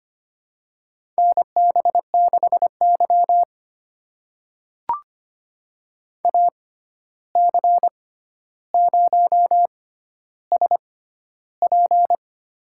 Morse code, heard as a keyed tone, about 25 wpm, spaced as for 12 wpm, 700 Hz. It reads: NB6Y AC0SP
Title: USA Call Signs - 6x Character Spacing - 25wpm